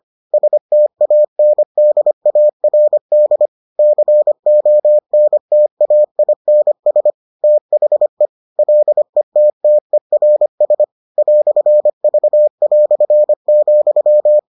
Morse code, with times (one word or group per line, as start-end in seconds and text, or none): 0.3-3.5 STANDARD
3.8-7.1 CONTAINS
7.4-8.3 THE
8.6-10.8 LETTERS
11.2-14.4 "V",